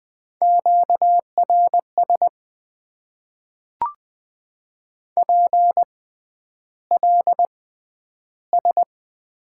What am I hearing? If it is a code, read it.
Morse code, heard as a keyed tone, about 20 wpm, spaced as for 10 wpm, 700 Hz. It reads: QRS PLS
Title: Top 90 QSO Elements - 6x Character Spacing - 20wpm